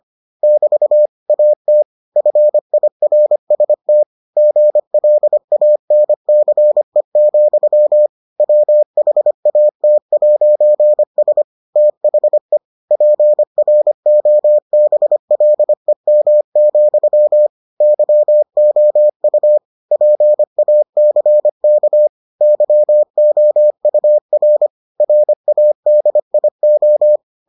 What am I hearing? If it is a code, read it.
Morse code, heard as a keyed tone, about 25 wpm, 600 Hz. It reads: <BT> AT FIRST GLANCE, WHAT'S THE PROBLEM, YOU PACK YOUR RADIO